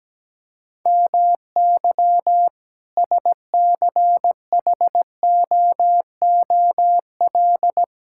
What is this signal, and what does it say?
Morse code, heard as a keyed tone, about 17 wpm, 700 Hz. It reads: MY SCHOOL
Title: Sentences from Top 1000 Words - Review Part 6 - 17wpm